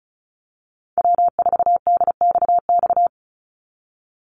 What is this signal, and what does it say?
Morse code, heard as a keyed tone, about 34 wpm, 700 Hz. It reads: W4DXX
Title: Call Signs Repeated 2x - 35wpm